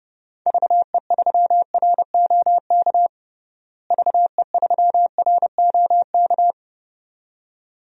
Morse code, heard as a keyed tone, about 30 wpm, 700 Hz. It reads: VE3ROK VE3ROK
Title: DXCC-100 Call Signs - 6 Characters - Repeated 2x - 30wpm